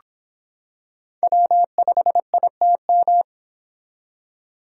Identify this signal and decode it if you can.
Morse code, heard as a keyed tone, about 23 wpm, 700 Hz. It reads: W5ITM